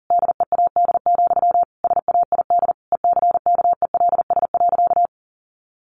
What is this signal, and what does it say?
Morse code, heard as a keyed tone, about 40 wpm, 700 Hz. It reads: DEAD, SAID ECKELS.
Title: A Sound Of Thunder 40wpm